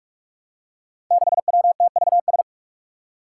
Morse code, heard as a keyed tone, about 44 wpm, 700 Hz. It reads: 6WTVS
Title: Five Challenging Letter-Number Combinations 45wpm